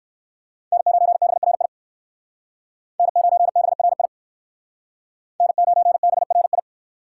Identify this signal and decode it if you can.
Morse code, heard as a keyed tone, about 59 wpm, 700 Hz. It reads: N9BRI N9BRI N9BRI